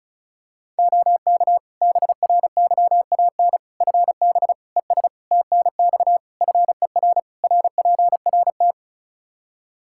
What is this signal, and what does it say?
Morse code, heard as a keyed tone, about 36 wpm, 700 Hz. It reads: OK BRYAN FB ES TNX FER RPRT
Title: Practice QSO - Exchange 2 - Info - 35wpm